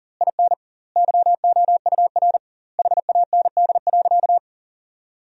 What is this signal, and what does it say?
Morse code, heard as a keyed tone, about 40 wpm, 700 Hz. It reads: IN YOUR HAND.
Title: 2x Word-Spacing - Sentences from Top 400 Words - 40wpm